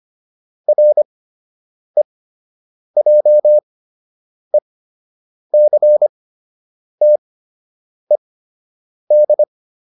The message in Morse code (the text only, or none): REJECTED